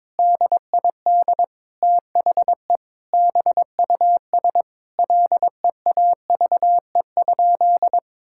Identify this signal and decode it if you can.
Morse code, heard as a keyed tone, about 22 wpm, 700 Hz. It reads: DID THE BUS LEAVE?
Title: Sentences from Top 1000 Words - Review Part 6 - 22wpm